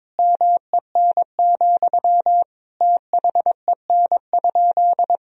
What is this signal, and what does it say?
Morse code, heard as a keyed tone, about 22 wpm, 700 Hz. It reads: MEN, THEN?
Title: Sentences from Top 1000 Words - Review Part 1 - 22wpm